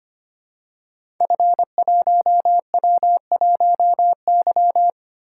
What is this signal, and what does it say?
Morse code, heard as a keyed tone, about 25 wpm, 700 Hz. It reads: F1W1Y
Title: Code Groups - 5 Letter:Numbers - 25wpm